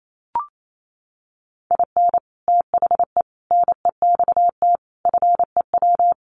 Morse code, heard as a keyed tone, about 28 wpm, 700 Hz. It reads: IN THE NEXT FEW
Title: Sets of 5 Words Encore2 28wpm